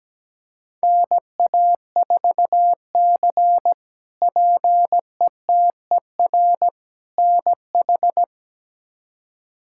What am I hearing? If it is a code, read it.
Morse code, heard as a keyed tone, about 17 wpm, 700 Hz. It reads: NA4C PETER NH
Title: K1USN SST QSO Practice - 17wpm